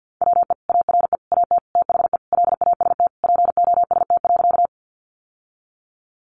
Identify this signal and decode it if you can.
Morse code, heard as a keyed tone, about 50 wpm, 700 Hz. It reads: WE ARE AT THE LAST POST.
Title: Sentences from Top 1000 Words 50wpm